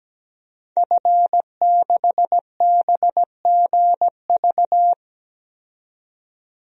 Morse code, heard as a keyed tone, about 17 wpm, 700 Hz. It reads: F6BGV
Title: DXCC-100 Call Signs - 5 Characters - 17wpm